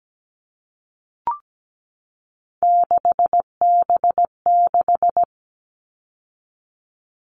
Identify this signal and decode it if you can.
Morse code, heard as a keyed tone, about 17 wpm, 700 Hz. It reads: 6B6
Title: All Cut Numbers - 3-Digit 17wpm